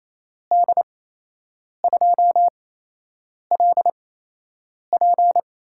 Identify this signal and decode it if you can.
Morse code, heard as a keyed tone, about 28 wpm, spaced as for 8 wpm, 700 Hz. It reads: D2LP
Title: USA Call Signs - 6 Characters - 8x Character Spacing - 28wpm